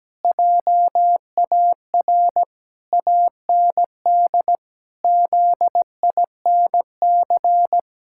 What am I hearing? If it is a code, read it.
Morse code, heard as a keyed tone, about 17 wpm, 700 Hz. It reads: JAR AND ZINC
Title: Pangram Challenge v2 - 17wpm